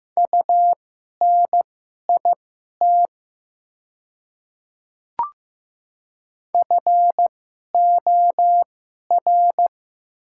Morse code, heard as a keyed tone, about 15 wpm, spaced as for 11 wpm, 700 Hz. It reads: UNIT FOR
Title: 2x Character Spacing - Top 500 Words - 15wpm